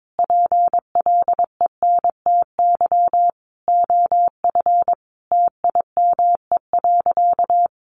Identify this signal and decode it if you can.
Morse code, heard as a keyed tone, about 22 wpm, 700 Hz. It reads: PLENTY OF TIME.
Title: Sentences from Top 1300 Words 22wpm